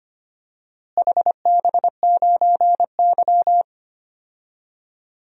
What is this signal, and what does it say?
Morse code, heard as a keyed tone, about 25 wpm, 700 Hz. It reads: HB9Y
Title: DXCC-100 Call Signs - 4 Characters - Repeated 3x - 25wpm